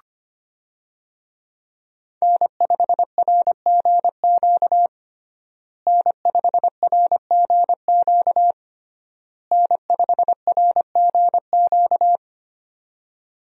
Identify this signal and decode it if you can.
Morse code, heard as a keyed tone, about 25 wpm, 700 Hz. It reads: N5RGQ N5RGQ N5RGQ
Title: USA Call Signs - 5 Characters - Repeated 3x - 25wpm